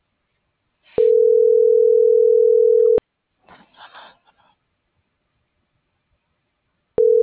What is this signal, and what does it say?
Ringback tone (North America), 440 + 480 Hz.